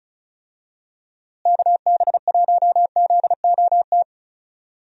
Morse code, heard as a keyed tone, about 36 wpm, 700 Hz. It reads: KB1ZOT
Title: USA Call Signs - 6 Characters - Repeated 3x - 35wpm